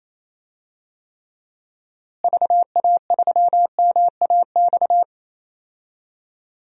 Morse code, heard as a keyed tone, about 27 wpm, 700 Hz. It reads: VA3MAX
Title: DXCC-100 Call Signs - 6 Characters - Repeated 2x - 28wpm